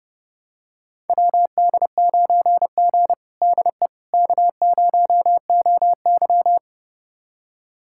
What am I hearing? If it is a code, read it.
Morse code, heard as a keyed tone, about 30 wpm, 700 Hz. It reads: WD9G DE K0OY
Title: USA Callsign DE Callsign - 4 Characters - 30wpm